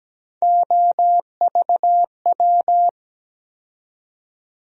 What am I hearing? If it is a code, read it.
Morse code, heard as a keyed tone, about 17 wpm, 700 Hz. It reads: OVW
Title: USA Call Sign Postfixes - 3 Letters 17wpm